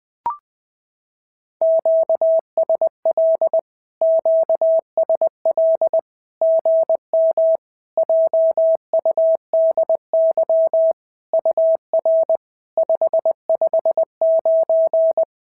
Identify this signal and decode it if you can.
Morse code, heard as a keyed tone, about 20 wpm, 650 Hz. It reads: QSL QSL GM JUDY UR 559
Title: SOTA QSO Practice 20wpm